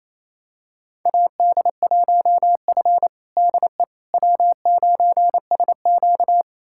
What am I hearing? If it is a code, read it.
Morse code, heard as a keyed tone, about 28 wpm, 700 Hz. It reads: AD1F DE W9SQ
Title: USA Callsign DE Callsign - 4 Characters - 28wpm